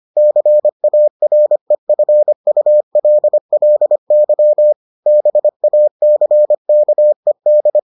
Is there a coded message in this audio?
Yes: Morse code from 0.2 to 7.8 s, heard as a keyed tone, about 25 wpm, 600 Hz. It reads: CAREFULLY BACKED